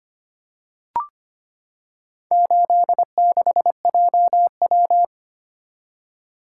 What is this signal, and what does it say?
Morse code, heard as a keyed tone, about 25 wpm, 700 Hz. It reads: 86JW
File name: practice / 25wpm / Four Letter-Number Combinations - 25wpm